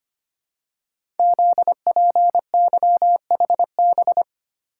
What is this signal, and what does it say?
Morse code, heard as a keyed tone, about 25 wpm, 700 Hz. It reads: ZPYHB
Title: Code Groups - 5 Letter:Numbers - 25wpm